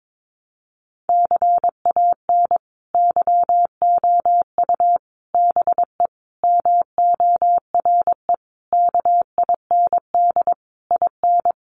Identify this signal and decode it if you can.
Morse code, heard as a keyed tone, about 22 wpm, 700 Hz. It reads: CAN YOU BE MORE KIND IN
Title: Sentences from Top 1100 Words 22wpm